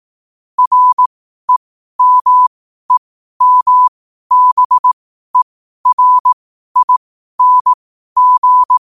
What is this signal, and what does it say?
Morse code, heard as a keyed tone, about 18 wpm, spaced as for 12 wpm, 1000 Hz. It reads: REMEMBERING
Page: 12-wpm-morse-code-quotes